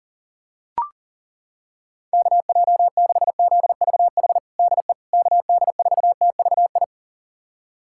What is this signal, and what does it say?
Morse code, heard as a keyed tone, about 40 wpm, 700 Hz. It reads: KJ6ZVH DE KD4TVI
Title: USA Callsign DE Callsign 40wpm